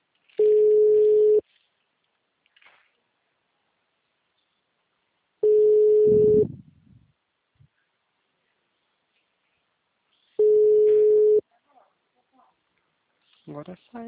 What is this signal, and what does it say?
Ringback tone (Europe, 425 Hz), with speech.